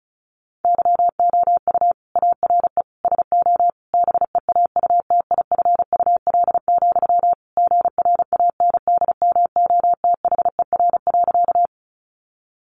Morse code, heard as a keyed tone, about 34 wpm, 700 Hz. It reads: YOU ARE SO BEAUTIFUL, GRANDMOTHER.
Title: Sentences from Top 1200 Words 35wpm